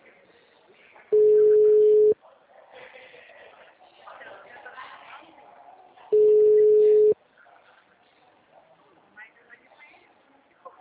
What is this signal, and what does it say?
Ringback tone (Europe, 425 Hz), with speech.
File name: out- -2006- - - .42812